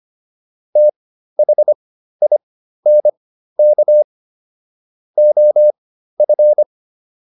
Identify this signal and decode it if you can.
Morse code, heard as a keyed tone, about 25 wpm, spaced as for 13 wpm, 600 Hz. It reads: THINK OF